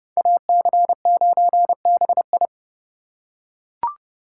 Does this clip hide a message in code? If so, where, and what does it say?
Yes: Morse code from 0.2 to 2.5 s, heard as a keyed tone, about 31 wpm, 700 Hz. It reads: AC9BI